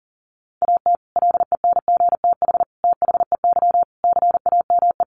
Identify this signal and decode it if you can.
Morse code, heard as a keyed tone, about 40 wpm, 700 Hz. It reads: AT LENGTH THEY CAME
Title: Sentences from Top 1000 Words 40wpm